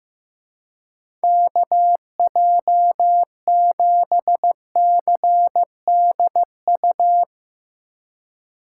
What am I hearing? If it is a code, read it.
Morse code, heard as a keyed tone, about 15 wpm, 700 Hz. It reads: KJ7CDU
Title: USA Call Signs - 6 Characters - 2x Character Spacing - 15wpm